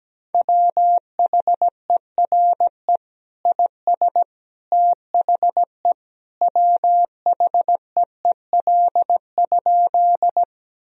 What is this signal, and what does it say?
Morse code, heard as a keyed tone, about 17 wpm, 700 Hz. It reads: WHERE IS THE WHEEL?
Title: Sentences from Top 1000 Words - Review Part 3 - 17wpm